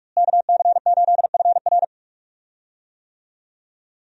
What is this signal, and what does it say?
Morse code, heard as a keyed tone, about 41 wpm, 700 Hz. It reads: KC8FR
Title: USA Call Signs - 5 Characters - Repeated 2x - 45wpm